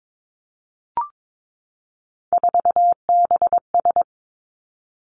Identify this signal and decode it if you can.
Morse code, heard as a keyed tone, about 22 wpm, 700 Hz. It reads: 4BS
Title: Three Challenging Letter-Number Combinations 22wpm